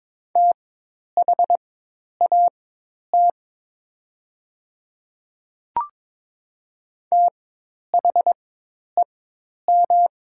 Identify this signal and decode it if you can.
Morse code, heard as a keyed tone, about 22 wpm, spaced as for 10 wpm, 700 Hz. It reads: THAT THEM
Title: 4x Character Spacing - Top 100 Words - 22wpm